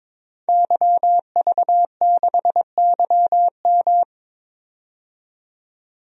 Morse code, heard as a keyed tone, about 22 wpm, 700 Hz. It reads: YV6YM
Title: DXCC-100 Call Signs - 5 Characters - 22wpm